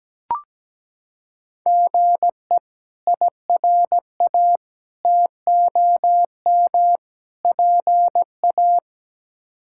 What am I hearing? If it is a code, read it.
Morse code, heard as a keyed tone, about 17 wpm, 700 Hz. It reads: GE IRA TOM PA
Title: K1USN SST QSO Practice - 17wpm